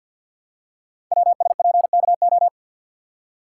Morse code, heard as a keyed tone, about 50 wpm, 700 Hz. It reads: WSPKO
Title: Code Groups - 5 Letters - 50wpm